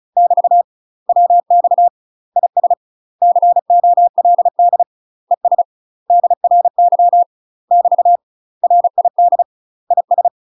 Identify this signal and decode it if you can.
Morse code, heard as a keyed tone, about 34 wpm, 700 Hz. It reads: <BT> WX IS COLD ES DRY <BT> RID IS
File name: UK MASTER QSOs TRAINING/35WPM